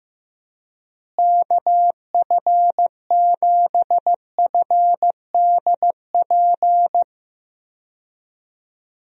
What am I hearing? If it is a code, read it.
Morse code, heard as a keyed tone, about 15 wpm, 700 Hz. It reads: KF7FDP
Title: USA Call Signs - 6 Characters - Repeated 3x - 15wpm